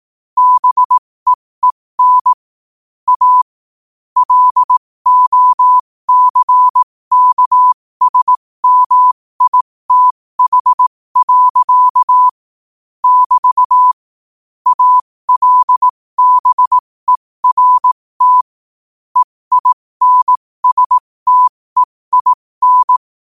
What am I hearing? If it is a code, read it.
Morse code, heard as a keyed tone, about 18 wpm, spaced as for 15 wpm, 1000 Hz. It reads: BEEN A LOCKSMITH. <BT> ALBERT EINSTEIN
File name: qotdfiles / 15wpm